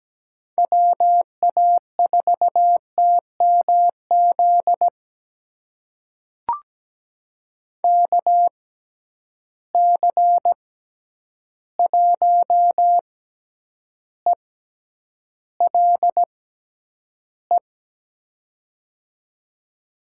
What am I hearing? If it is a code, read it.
Morse code, heard as a keyed tone, about 17 wpm, spaced as for 9 wpm, 700 Hz. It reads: WA4TMZ KC1ELE
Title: USA Call Signs - 6 Characters - 6x Character Spacing - 17wpm